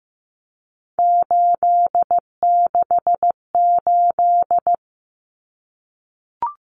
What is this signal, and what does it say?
Morse code, heard as a keyed tone, about 15 wpm, 700 Hz. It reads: 868
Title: Numbers - 3-Digit 15wpm